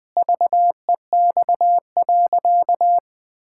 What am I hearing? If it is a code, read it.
Morse code, heard as a keyed tone, about 20 wpm, 700 Hz. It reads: VEX.
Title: Pangram Challenge v2 - 20wpm